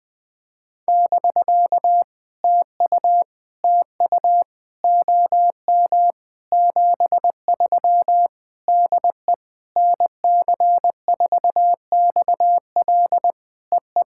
Morse code, heard as a keyed tone, about 19 wpm, 700 Hz. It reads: <BK> TU TU OM 73 DE NC4XL EE
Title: POTA QSO Practice - 20wpm